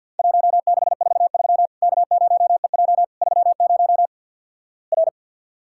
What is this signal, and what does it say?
Morse code, heard as a keyed tone, about 50 wpm, 700 Hz. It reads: 1643 K0EJ 30 R
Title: Worked All Europe DX Contest - QTC Exchange - 50wpm